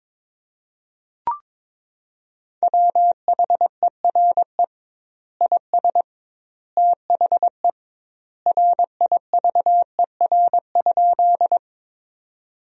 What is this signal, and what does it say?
Morse code, heard as a keyed tone, about 22 wpm, 700 Hz. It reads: WHERE IS THE RIVER?
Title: 2x Word-Spacing - Sentences from Top 300 Words - 22wpm